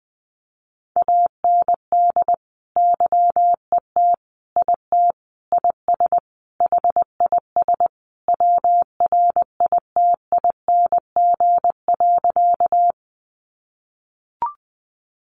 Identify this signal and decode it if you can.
Morse code, heard as a keyed tone, about 20 wpm, 700 Hz. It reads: AND YET IT IS HIS WRITING.
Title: Sentences from Top 800 Words 20wpm